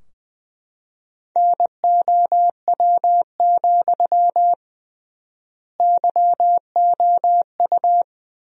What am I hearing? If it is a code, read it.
Morse code, heard as a keyed tone, about 20 wpm, 700 Hz. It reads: NOW, YOU
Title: 3x Word-Spacing - Sentences from Top 200 Words - 20wpm